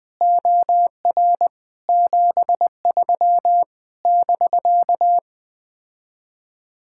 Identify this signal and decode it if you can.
Morse code, heard as a keyed tone, about 20 wpm, 700 Hz. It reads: OR 73 <BK>
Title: Speed-Racing Common QSO Phrases Encore - 17wpm